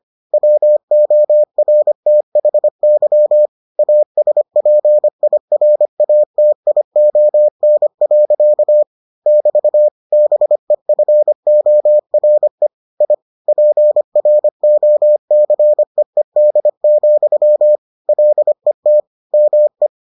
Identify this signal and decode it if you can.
Morse code, heard as a keyed tone, about 25 wpm, 600 Hz. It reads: WORTHY ASPIRATION. <BT> BEFORE I PROCEED, LET ME